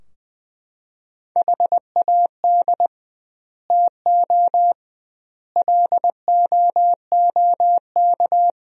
Morse code, heard as a keyed tone, about 20 wpm, 700 Hz. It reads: HAD TO LOOK
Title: 2x Word-Spacing - Sentences from Top 100 Words - 20wpm